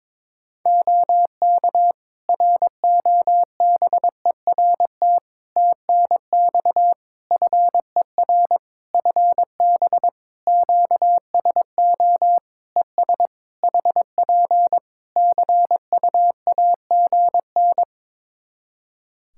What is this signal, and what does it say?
Morse code, heard as a keyed tone, about 22 wpm, 700 Hz. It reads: OK ROBERT TNX FER FB QSO ES HP CUAGN
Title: K1USN SST QSO Practice 22wpm